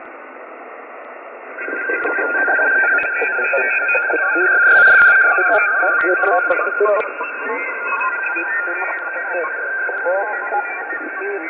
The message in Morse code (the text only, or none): none